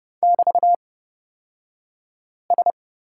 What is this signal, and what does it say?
Morse code, heard as a keyed tone, about 30 wpm, spaced as for 5 wpm, 700 Hz. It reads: <BT>S
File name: Lesson 37d - Letter x - Two Characters 30wpm